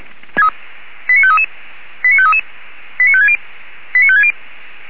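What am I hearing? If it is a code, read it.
Five-tone selective call (ZVEI), heard as quick runs of five tones, tones of about 70 ms. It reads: 89533 89533 89560 89560
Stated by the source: ZVEI five-tone selective call received by SDR.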